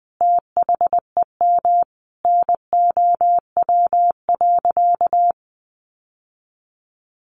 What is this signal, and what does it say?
Morse code, heard as a keyed tone, about 20 wpm, 700 Hz. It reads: THEM NOW.
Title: Sentences from Top 1300 Words 20wpm